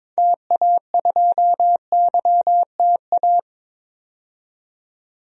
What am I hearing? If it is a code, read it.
Morse code, heard as a keyed tone, about 22 wpm, 700 Hz. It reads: TA2YTA
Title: DXCC-100 Call Signs - 6 Characters - 22wpm